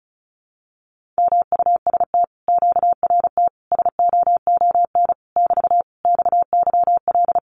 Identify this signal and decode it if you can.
Morse code, heard as a keyed tone, about 36 wpm, 700 Hz. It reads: MUST QRT SOON <BT> XYL
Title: Common QSO Phrases Encore 35wpm